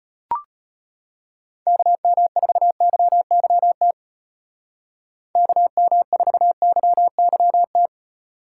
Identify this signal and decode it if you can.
Morse code, heard as a keyed tone, about 30 wpm, 700 Hz. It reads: KM4YYT KM4YYT
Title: Speed Racing - USA 2x3 Call Signs 25wpm